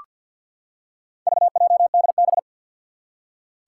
Morse code, heard as a keyed tone, about 47 wpm, 700 Hz. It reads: VJDB
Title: Four Challenging Letter-Number Combinations 50wpm